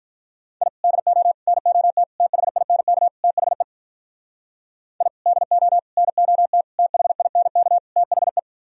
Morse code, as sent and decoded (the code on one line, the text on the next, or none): .. / -.. --- / -. --- - / - .... .. -. -.- / - .... / .. / -.. --- / -. --- - / - .... .. -. -.- / - .... .
I DO NOT THINK THE I DO NOT THINK THE